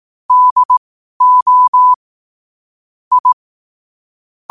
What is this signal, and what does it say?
Morse code, heard as a keyed tone, about 18 wpm, spaced as for 12 wpm, 1000 Hz. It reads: DO I
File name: qotdfiles / 12wpm